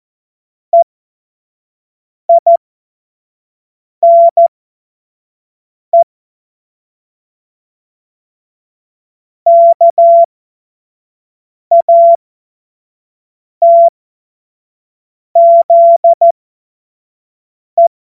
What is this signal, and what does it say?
Morse code, heard as a keyed tone, about 14 wpm, spaced as for 5 wpm, 680 Hz. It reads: EINE KATZE
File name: Morsekurs / Witze / Witz 05. April 2022 25 F70